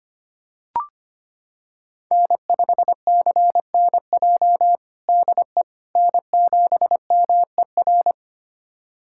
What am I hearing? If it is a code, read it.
Morse code, heard as a keyed tone, about 25 wpm, 700 Hz. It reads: N5CNJ DE N7MER
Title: USA Callsign DE Callsign - 5 Characters - 25wpm